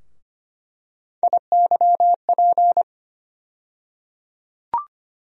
Morse code, heard as a keyed tone, about 25 wpm, 700 Hz. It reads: IYP
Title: USA Call Sign Postfixes - 3 Letters 25wpm